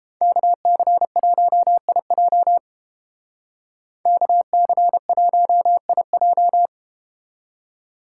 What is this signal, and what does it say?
Morse code, heard as a keyed tone, about 32 wpm, 700 Hz. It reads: KC1IJ KC1IJ